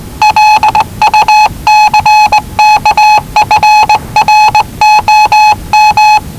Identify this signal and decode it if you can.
Morse code, heard as a keyed tone, about 19 wpm, 850 Hz. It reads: LUCK FROM
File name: WellingtonRadioZlwCloseDownOnHfAndMf / 01-Zlw 4277khz 417.5 qrt until 0925z